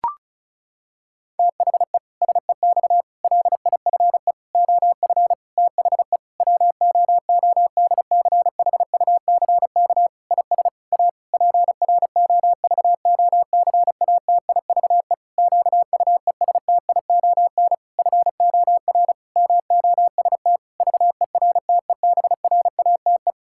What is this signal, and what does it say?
Morse code, heard as a keyed tone, about 34 wpm, 700 Hz. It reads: THE SEX LIFE OF THE WOODCHUCK IS A PROVOCATIVE QUESTION FOR MOST VERTEBRATE
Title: Pangram Challenge v2 - 35wpm